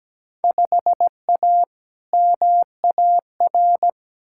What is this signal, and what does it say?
Morse code, heard as a keyed tone, about 17 wpm, 700 Hz. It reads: 5A MAR
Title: Field Day Exchanges Encore - 17wpm